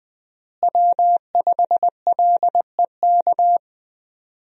Morse code, heard as a keyed tone, about 21 wpm, 700 Hz. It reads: W5LEK
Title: USA Call Signs - 5 Characters - Repeated 3x - 20wpm